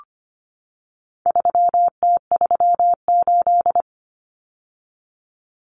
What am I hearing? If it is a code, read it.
Morse code, heard as a keyed tone, about 25 wpm, 700 Hz. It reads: 3T38
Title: Common Cut Numbers - 4-Digit 25wpm